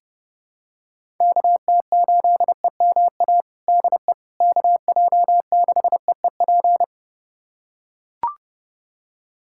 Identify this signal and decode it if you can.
Morse code, heard as a keyed tone, about 30 wpm, 700 Hz. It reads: KT8EMA DE KJ6EEP
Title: USA Callsign DE Callsign 30wpm